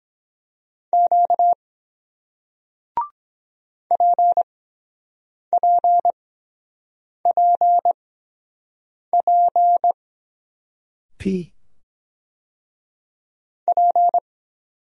Morse code, heard as a keyed tone, about 23 wpm, spaced as for 7 wpm, 700 Hz. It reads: Q PPPP P